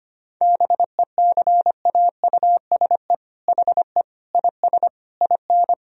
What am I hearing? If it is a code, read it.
Morse code, heard as a keyed tone, about 25 wpm, 700 Hz. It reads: BECAUSE HE IS IN
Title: Sentences from Top 1000 Words - Review Part 3 - 25wpm